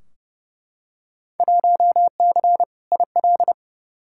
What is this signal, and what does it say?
Morse code, heard as a keyed tone, about 30 wpm, 700 Hz. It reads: 1C IL
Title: Field Day Exchanges - 30wpm